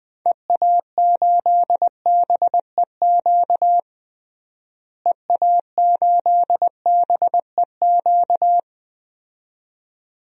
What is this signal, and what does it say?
Morse code, heard as a keyed tone, about 20 wpm, 700 Hz. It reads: EA8BEQ EA8BEQ
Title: DXCC-100 Call Signs - 6 Characters - Repeated 2x - 20wpm